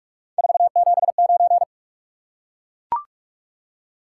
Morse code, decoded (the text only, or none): 479